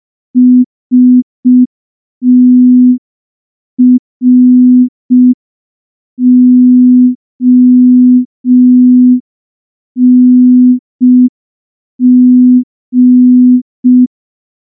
Morse code, heard as a keyed tone, about 5 wpm, 250 Hz. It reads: STRONG
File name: morse / morse-training-05WPM